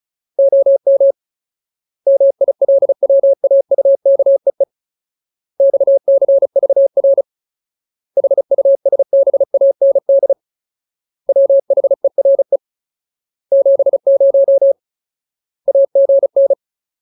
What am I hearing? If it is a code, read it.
Morse code, heard as a keyed tone, about 34 wpm, 550 Hz. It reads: OM MILWAUKEE XCVR HUSBAND WHERE 70 AGN
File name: Ultimate Word List / 35WPM Ultimate